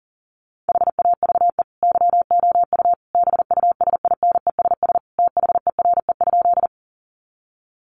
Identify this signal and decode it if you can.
Morse code, heard as a keyed tone, about 40 wpm, 700 Hz. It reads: HAVE YOU BUSINESS THERE?
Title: Sentences from Top 600 Words 40wpm